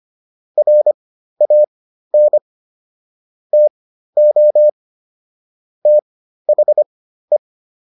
Morse code, heard as a keyed tone, about 25 wpm, spaced as for 13 wpm, 600 Hz. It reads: RAN TO THE